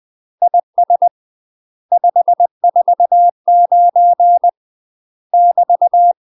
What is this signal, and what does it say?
Morse code, heard as a keyed tone, about 20 wpm, 700 Hz. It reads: IS 549 <BT>
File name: UK MASTER QSOs TRAINING / 20WPM / UK MASTER QSOs 0046